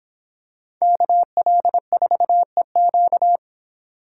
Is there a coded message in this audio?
Yes: Morse code from 0.8 to 3.4 s, heard as a keyed tone, about 26 wpm, 700 Hz. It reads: KL4EQ